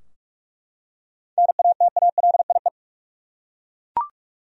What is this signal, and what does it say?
Morse code, heard as a keyed tone, about 46 wpm, 700 Hz. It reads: NATALIE
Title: USA Common Names - 45wpm